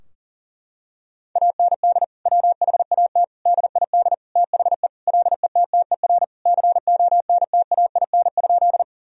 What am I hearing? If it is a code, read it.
Morse code, heard as a keyed tone, about 40 wpm, 700 Hz. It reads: AND WHAT DID THE LETTER CONTAIN?
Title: Sentences from Top 1000 Words - Review Part 2 - 40wpm